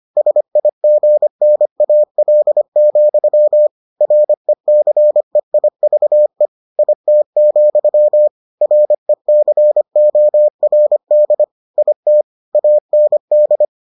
Morse code, heard as a keyed tone, about 25 wpm, 600 Hz. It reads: SIGNAL, RECEIVE IT, RECORD IT AND